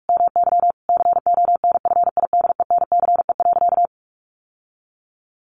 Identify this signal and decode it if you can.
Morse code, heard as a keyed tone, about 44 wpm, 700 Hz. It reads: MY CONFIDENCE.
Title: Sentences from Top 1500 Words 45wpm